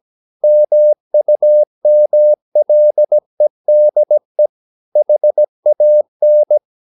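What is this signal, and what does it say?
Morse code, heard as a keyed tone, about 17 wpm, 600 Hz. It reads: MUMLEDE HAN
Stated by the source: Slow (QRS) Morse practice at 17 wpm, from CW med Gnister.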